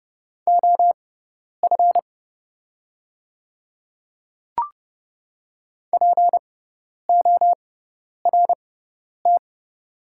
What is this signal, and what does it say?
Morse code, heard as a keyed tone, about 30 wpm, spaced as for 10 wpm, 700 Hz. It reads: OF PORT
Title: 6x Character Spacing - Top 200 Words - 30wpm